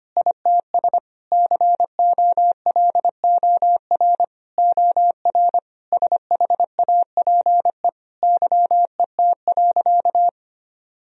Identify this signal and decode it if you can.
Morse code, heard as a keyed tone, about 25 wpm, 700 Hz. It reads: ITS COLOR OR SHAPE YET.
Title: Sentences from Top 1000 Words - Review Part 3 - 25wpm